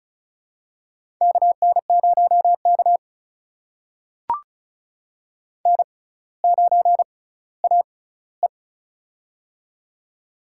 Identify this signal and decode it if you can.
Morse code, heard as a keyed tone, about 34 wpm, spaced as for 18 wpm, 700 Hz. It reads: KN0K N9AE
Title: USA Call Signs - 4 Characters - 6x Character Spacing - 35wpm